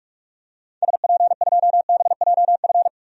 Morse code, heard as a keyed tone, about 46 wpm, 700 Hz. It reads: SP2BJF
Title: DXCC-100 Call Signs - 6 Characters - 45wpm